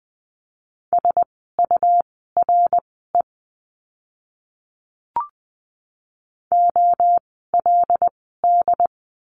Morse code, heard as a keyed tone, about 20 wpm, spaced as for 15 wpm, 700 Hz. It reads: SURE OLD
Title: 101 Lesson 17f - Letter u - Words 20wpm